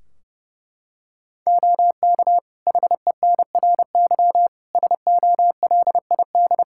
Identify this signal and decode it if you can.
Morse code, heard as a keyed tone, about 30 wpm, 700 Hz. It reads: OK HENRY SOLID
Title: Practice QSO - Exchange 3 - More Info - 30wpm